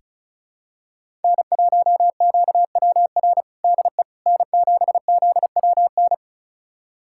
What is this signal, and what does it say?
Morse code, heard as a keyed tone, about 36 wpm, 700 Hz. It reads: N1QWR DE N7ZWN
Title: USA Callsign DE Callsign - 5 Characters - 35wpm